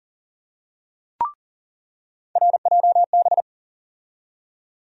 Morse code, heard as a keyed tone, about 38 wpm, 700 Hz. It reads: RJB
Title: USA Call Sign Postfixes - 3 Letters 40wpm